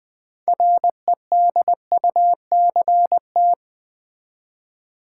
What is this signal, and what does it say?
Morse code, heard as a keyed tone, about 20 wpm, 700 Hz. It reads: REDUCT
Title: Build A Word - 3 Letters At A Time - 9-Letter Words - 20wpm